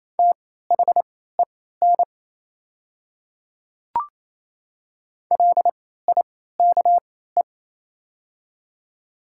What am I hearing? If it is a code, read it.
Morse code, heard as a keyed tone, about 27 wpm, spaced as for 16 wpm, 700 Hz. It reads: THEN LIKE
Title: 3x Character Spacing - Top 200 Words - 28wpm